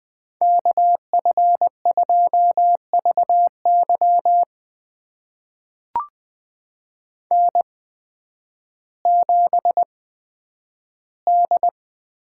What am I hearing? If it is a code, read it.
Morse code, heard as a keyed tone, about 20 wpm, spaced as for 11 wpm, 700 Hz. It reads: KF2VY N7D
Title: USA Call Signs - 5 Characters - 8x Character Spacing - 20wpm